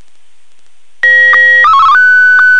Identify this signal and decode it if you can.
SSTV (slow-scan television), Martin 1.